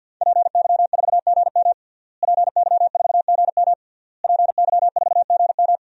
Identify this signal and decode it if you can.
Morse code, heard as a keyed tone, about 52 wpm, 700 Hz. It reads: PY4GM PY4GM PY4GM
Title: DXCC-100 Call Signs - 5 Characters - Repeated 3x - 50wpm